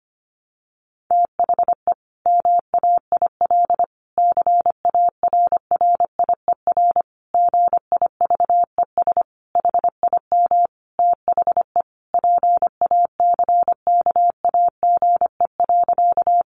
Morse code, heard as a keyed tone, about 25 wpm, 700 Hz. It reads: THE MAIL CARRIER GIVES HIM THE PACKAGE.
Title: Additional Easy Phrases 25wpm